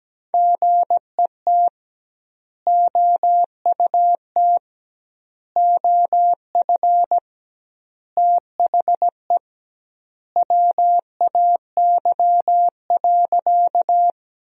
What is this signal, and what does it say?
Morse code, heard as a keyed tone, about 17 wpm, 700 Hz. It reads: GET OUT OF THE WAY.
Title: 2x Word-Spacing - Sentences from Top 200 Words - 17wpm